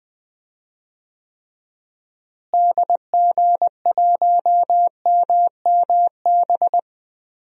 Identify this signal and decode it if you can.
Morse code, heard as a keyed tone, about 20 wpm, 700 Hz. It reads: DG1MMB
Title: DXCC-100 Call Signs - 6 Characters - Repeated 2x - 20wpm